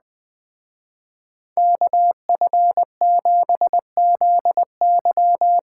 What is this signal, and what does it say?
Morse code, heard as a keyed tone, about 20 wpm, 700 Hz. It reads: KF7ZY
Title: Speed Racing - USA 2x2 Call Signs 17wpm